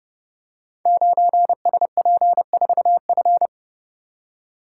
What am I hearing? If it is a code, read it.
Morse code, heard as a keyed tone, about 29 wpm, 700 Hz. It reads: 9SP4F
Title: Code Groups - 5 Letter:Numbers - 30wpm